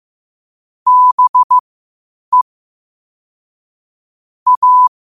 Morse code, heard as a keyed tone, about 15 wpm, spaced as for 8 wpm, 1000 Hz. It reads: BE A
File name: qotdfiles / 8wpm